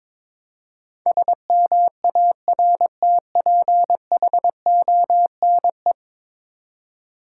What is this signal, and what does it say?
Morse code, heard as a keyed tone, about 22 wpm, 700 Hz. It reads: SMARTPHONE
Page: morse-code-ninja-practice-22wpm